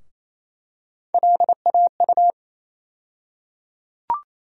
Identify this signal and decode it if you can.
Morse code, heard as a keyed tone, about 28 wpm, 700 Hz. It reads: LAU
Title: USA Call Sign Postfixes - 3 Letters 28wpm